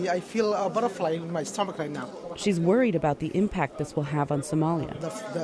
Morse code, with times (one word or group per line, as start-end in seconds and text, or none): none